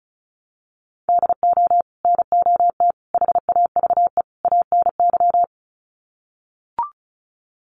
Morse code, heard as a keyed tone, about 34 wpm, 700 Hz. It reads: DO NOT HAVE ANY
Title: Sets of 4 Words 35wpm